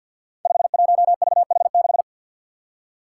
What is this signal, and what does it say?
Morse code, heard as a keyed tone, about 53 wpm, 700 Hz. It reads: 51VH6
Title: Five Challenging Letter-Number Combinations 50wpm